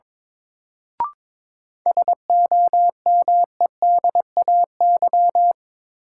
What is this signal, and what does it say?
Morse code, heard as a keyed tone, about 22 wpm, 700 Hz. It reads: SOMEDAY
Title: Sentences from Top 1600 Words - New Word Drill Alt - 22wpm